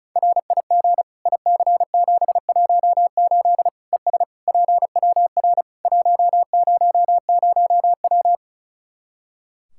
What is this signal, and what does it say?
Morse code, heard as a keyed tone, about 36 wpm, 700 Hz. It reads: RIG IC718 ES PWR 100W
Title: Practice QSO - Exchange 2 - Info - 35wpm